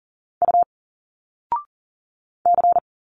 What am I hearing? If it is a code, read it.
Morse code, heard as a keyed tone, about 40 wpm, 700 Hz. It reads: U C
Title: Single Letters 40wpm